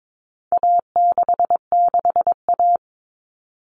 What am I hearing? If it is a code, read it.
Morse code, heard as a keyed tone, about 22 wpm, 700 Hz. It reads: A66A
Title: All Cut Numbers - 4-Digit 22wpm